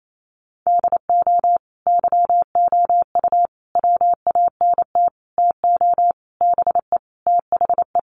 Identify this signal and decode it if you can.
Morse code, heard as a keyed tone, about 28 wpm, 700 Hz. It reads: DO YOU WANT TO BE THE
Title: Sentences from Top 1100 Words 28wpm